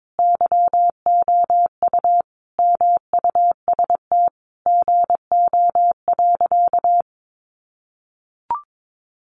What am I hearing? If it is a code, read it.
Morse code, heard as a keyed tone, about 22 wpm, 700 Hz. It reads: YOU MUST GO.
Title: Sentences from Top 1200 Words 22wpm